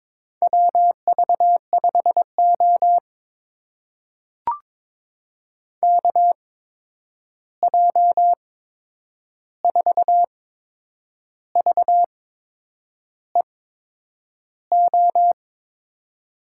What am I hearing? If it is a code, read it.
Morse code, heard as a keyed tone, about 22 wpm, spaced as for 8 wpm, 700 Hz. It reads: WV5O KJ4VEO
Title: USA Call Signs - 8x Character Spacing - 22wpm